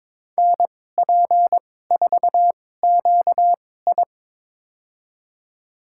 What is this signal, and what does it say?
Morse code, heard as a keyed tone, about 22 wpm, spaced as for 16 wpm, 700 Hz. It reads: NP4QI
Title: USA Call Signs - 2x Character Spacing - 22wpm